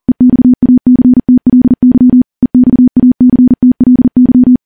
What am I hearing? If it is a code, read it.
Morse code, heard as a keyed tone, about 40 wpm, 250 Hz. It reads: EXACTLY EXACTLY
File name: morse / morse-training-40WPM